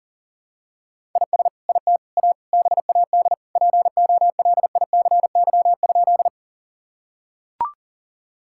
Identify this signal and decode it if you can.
Morse code, heard as a keyed tone, about 40 wpm, 700 Hz. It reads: IS IT A BAD POLICY?
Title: Sentences from Top 1000 Words - Review Part 5 - 40wpm